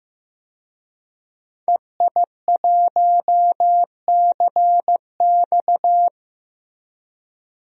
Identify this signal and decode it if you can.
Morse code, heard as a keyed tone, about 15 wpm, 700 Hz. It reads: EI1CX
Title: DXCC-100 Call Signs - 5 Characters - Repeated 2x - 15wpm